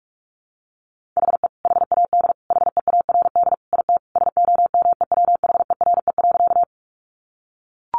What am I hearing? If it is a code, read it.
Morse code, heard as a keyed tone, about 44 wpm, 700 Hz. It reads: HE HAD HEARD IT SOMEWHERE.
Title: Sentences from Top 1400 Words 45wpm